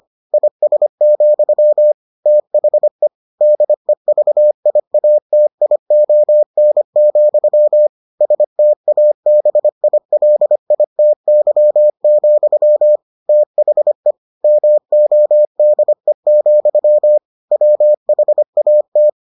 Morse code, heard as a keyed tone, about 25 wpm, 600 Hz. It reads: IS, THE DEVIATION, STABILITY, THE MODE, WHAT